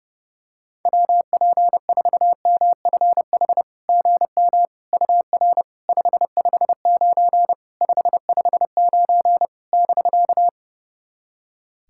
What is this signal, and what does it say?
Morse code, heard as a keyed tone, about 30 wpm, 700 Hz. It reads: WP4MFH GM UR 559 559 <BK>
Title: SOTA QSO Practice 30wpm